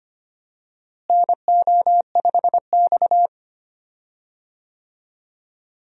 Morse code, heard as a keyed tone, about 25 wpm, 700 Hz. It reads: NO5X
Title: USA Call Signs - 4 Characters - Repeated 2x - 25wpm